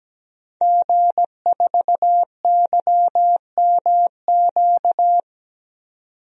Morse code, heard as a keyed tone, about 17 wpm, 700 Hz. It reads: G4YMQ
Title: DXCC-100 Call Signs - 5 Characters - 17wpm